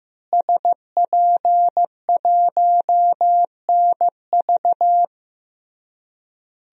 Morse code, heard as a keyed tone, about 15 wpm, 700 Hz. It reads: SP1NV
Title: DXCC-100 Call Signs - 5 Characters - Repeated 3x - 15wpm